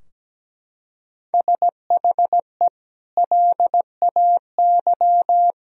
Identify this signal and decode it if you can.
Morse code, heard as a keyed tone, about 17 wpm, 700 Hz. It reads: SHE LAY
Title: Sentences from Top 1000 Words - Review Part 2 - 17wpm